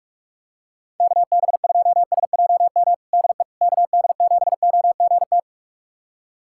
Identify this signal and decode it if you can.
Morse code, heard as a keyed tone, about 46 wpm, 700 Hz. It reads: KB2SJM DE KD7OGT
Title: USA Callsign DE Callsign - 6 Characters - 45wpm